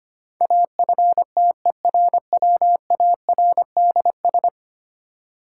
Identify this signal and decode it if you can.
Morse code, heard as a keyed tone, about 25 wpm, 700 Hz. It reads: AFTERWARDS
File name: 10 Letter Words - First 3 Letters Given - 25wpm